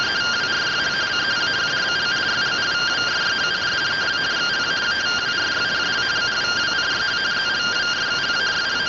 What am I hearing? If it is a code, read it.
Two-tone teleprinter signal, about 170 Hz shift.
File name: findings / AXIAL fr130 3mm 409.gen-16 rtty rx